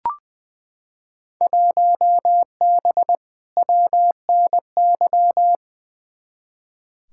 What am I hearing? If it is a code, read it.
Morse code, heard as a keyed tone, about 20 wpm, 700 Hz. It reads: 1B WNY
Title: Field Day Exchanges Encore - 20wpm